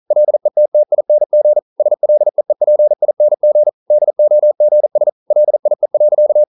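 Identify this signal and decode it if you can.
Morse code, heard as a keyed tone, about 42 wpm, 600 Hz. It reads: LETTING SLEEPING DOGS LIE.
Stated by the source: Morse code practice at 41 wpm, from CW med Gnister.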